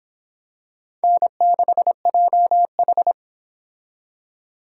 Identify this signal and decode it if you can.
Morse code, heard as a keyed tone, about 26 wpm, 700 Hz. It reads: N6JH